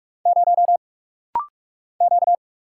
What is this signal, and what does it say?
Morse code, heard as a keyed tone, about 48 wpm, 700 Hz. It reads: 0 Q
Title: Single Letter-Number - Rapid-Fire 45wpm